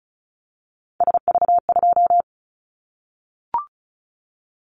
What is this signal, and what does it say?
Morse code, heard as a keyed tone, about 34 wpm, 700 Hz. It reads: SV2
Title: Callsign Prefixes for Top 50 DXCC - 35wpm